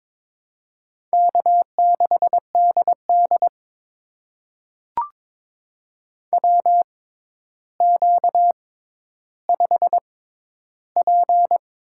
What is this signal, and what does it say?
Morse code, heard as a keyed tone, about 22 wpm, spaced as for 11 wpm, 700 Hz. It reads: K6DD WQ5P
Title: USA Call Signs - 4 Characters - 6x Character Spacing - 22wpm